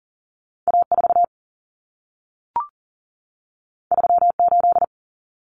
Morse code, heard as a keyed tone, about 40 wpm, 700 Hz. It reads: A4 38
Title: Common Cut Numbers - 2-Digit 40wpm